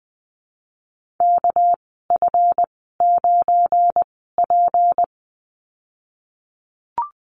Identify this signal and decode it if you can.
Morse code, heard as a keyed tone, about 20 wpm, spaced as for 15 wpm, 700 Hz. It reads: KF9P
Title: 219 Lesson 35g - Number 0 - Call Signs 20wpm